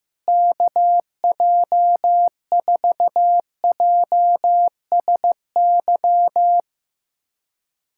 Morse code, heard as a keyed tone, about 15 wpm, 700 Hz. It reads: KJ4JSY
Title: USA Callsign DE Callsign - 6 Characters - 15wpm